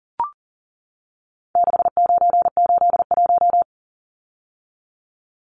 Morse code, heard as a keyed tone, about 40 wpm, 700 Hz. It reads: 6981
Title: Numbers - 4-Digit 40wpm